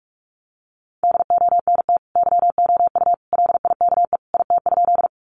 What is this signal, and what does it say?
Morse code, heard as a keyed tone, about 44 wpm, 700 Hz. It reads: DONT YOU LIKE IT?
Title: Sentences from Top 300 Words 45wpm